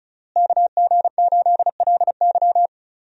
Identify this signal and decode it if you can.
Morse code, heard as a keyed tone, about 34 wpm, 700 Hz. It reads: KG8LY
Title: USA Call Signs - 5 Characters - Repeated 3x - 35wpm